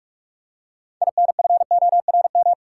Morse code, heard as a keyed tone, about 46 wpm, 700 Hz. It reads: INFORM
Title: Six Letter Words - 45wpm